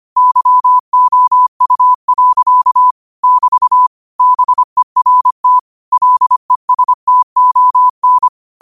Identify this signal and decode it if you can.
Morse code, heard as a keyed tone, about 25 wpm, 1000 Hz. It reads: YOU. <BT> BERT LESTON
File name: qotdfiles / 25wpm